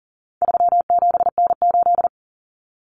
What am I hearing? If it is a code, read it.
Morse code, heard as a keyed tone, about 40 wpm, 700 Hz. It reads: 37N8
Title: Common Cut Numbers - 4-Digit 40wpm